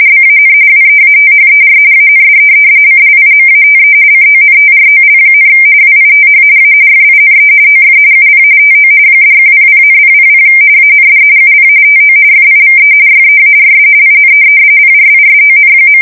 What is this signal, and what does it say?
RTTY, about 45.45 baud. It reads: BAN WEATHER AND CLIMATE CONTROL, WITH A FINE OF UP TO $100,000 AND UP TO FIVE YEARS IN PRISON.
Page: 170 Shift 2125/2295